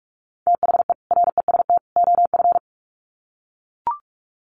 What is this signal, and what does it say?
Morse code, heard as a keyed tone, about 44 wpm, 700 Hz. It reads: THE REST OF
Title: Sets of 3 Words 45wpm